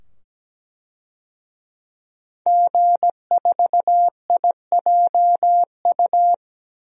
Morse code, heard as a keyed tone, about 17 wpm, 700 Hz. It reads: G4IJU
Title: DXCC-100 Call Signs - 5 Characters - 17wpm